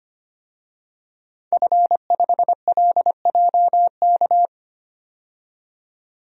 Morse code, heard as a keyed tone, about 25 wpm, 700 Hz. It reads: F5LJK
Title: DXCC-100 Call Signs - 5 Characters - 25wpm